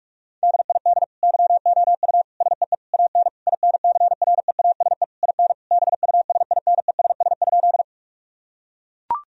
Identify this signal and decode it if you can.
Morse code, heard as a keyed tone, about 42 wpm, 700 Hz. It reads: DID YOU SEE AN INCREASE IN BUSINESS?
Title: Sentences from Top 1000 Words - Review Part 4 - 45wpm